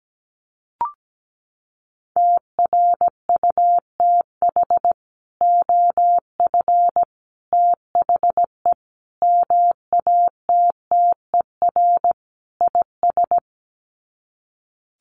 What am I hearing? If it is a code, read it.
Morse code, heard as a keyed tone, about 17 wpm, 700 Hz. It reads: TRUTH OF THE MATTER IS
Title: Sets of 5 Words Encore2 17wpm